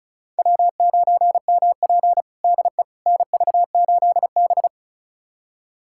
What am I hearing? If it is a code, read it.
Morse code, heard as a keyed tone, about 36 wpm, 700 Hz. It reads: W9MP DE NV8B
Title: USA Callsign DE Callsign - 4 Characters - 35wpm